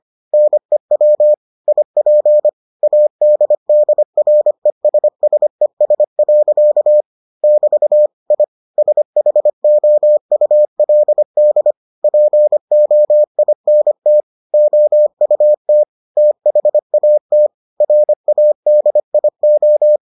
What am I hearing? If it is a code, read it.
Morse code, heard as a keyed tone, about 25 wpm, 600 Hz. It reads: NEW IP ADDRESSES. <BT> I SHOULD POINT OUT THAT RADIO